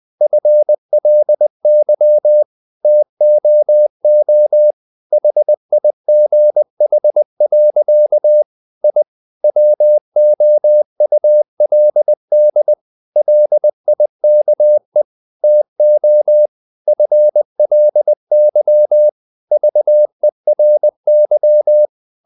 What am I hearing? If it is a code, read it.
Morse code, heard as a keyed tone, about 20 wpm, 600 Hz. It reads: FLY TOO HIGH. I WOULD LIKE TO FLY VERY
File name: the old man and the sea part 2 20wpm